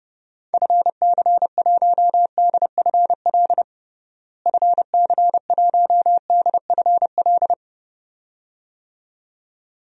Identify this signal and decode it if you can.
Morse code, heard as a keyed tone, about 32 wpm, 700 Hz. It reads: FC1DFL FC1DFL
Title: DXCC-100 Call Signs - 6 Characters - Repeated 2x - 30wpm